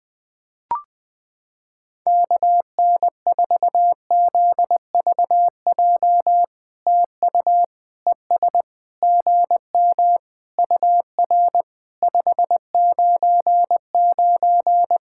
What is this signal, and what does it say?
Morse code, heard as a keyed tone, about 20 wpm, 700 Hz. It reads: KN4ZVJ TU ES GM UR 599
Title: POTA QSO Practice - 20wpm